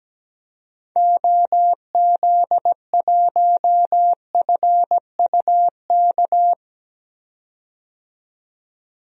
Morse code, heard as a keyed tone, about 17 wpm, 700 Hz. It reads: OZ1FUK
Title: DXCC-100 Call Signs - 6 Characters - 17wpm